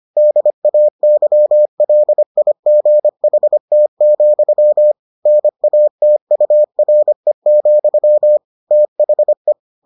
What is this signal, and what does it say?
Morse code, heard as a keyed tone, about 25 wpm, 600 Hz. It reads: DAYLIGHT, NATURE, THE